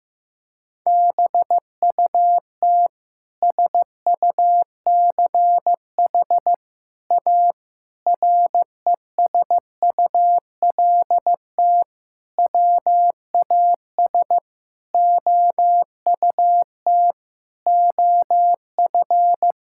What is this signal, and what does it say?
Morse code, heard as a keyed tone, about 15 wpm, 700 Hz. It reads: BUT SUCH A RESULT WAS OUT OF
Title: Sentences from Top 1000 Words - Review Part 5 - 15wpm